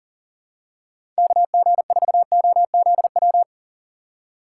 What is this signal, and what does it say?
Morse code, heard as a keyed tone, about 40 wpm, 690 Hz. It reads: KG4OZW
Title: USA Call Signs - 6 Characters - 2x Character Spacing - 40wpm